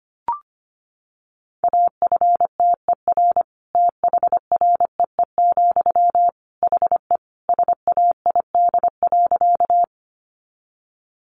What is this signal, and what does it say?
Morse code, heard as a keyed tone, about 25 wpm, 700 Hz. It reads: AFTER THREE, HE SAID.
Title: Sentences from Top 200 Words 25wpm